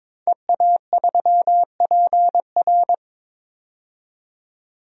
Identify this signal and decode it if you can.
Morse code, heard as a keyed tone, about 22 wpm, 700 Hz. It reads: EA3PR
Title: DXCC-100 Call Signs - 5 Characters - 22wpm